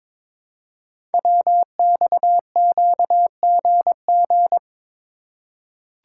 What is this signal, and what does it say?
Morse code, heard as a keyed tone, about 22 wpm, 700 Hz. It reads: WXQGG
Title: Code Groups - 5 Letters - 22wpm